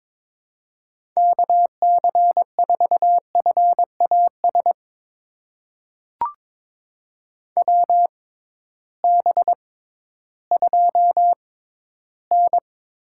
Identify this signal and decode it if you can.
Morse code, heard as a keyed tone, about 22 wpm, spaced as for 13 wpm, 700 Hz. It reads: KC4FAS WB2N
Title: USA Call Signs - 6 Characters - 6x Character Spacing - 22wpm